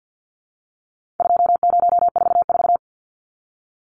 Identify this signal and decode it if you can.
Morse code, heard as a keyed tone, about 50 wpm, 700 Hz. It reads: 2044
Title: Numbers - 4-Digit 50wpm